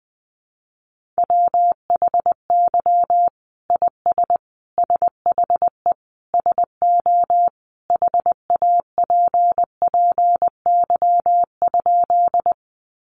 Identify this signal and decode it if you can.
Morse code, heard as a keyed tone, about 20 wpm, 700 Hz. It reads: WHY IS SHE SO HAPPY?
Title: Sentences from Top 1000 Words 20wpm